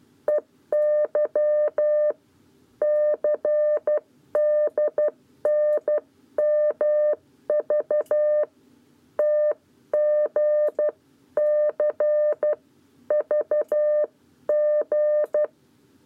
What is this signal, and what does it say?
Morse code, heard as a keyed tone, about 12 wpm, 600 Hz. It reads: EY CDNMV TG C VG
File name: 7b lezione cw 1